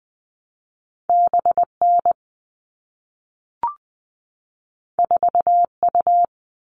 Morse code, heard as a keyed tone, about 20 wpm, 700 Hz. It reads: BN 4U
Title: All Cut Numbers - 2-Digit 20wpm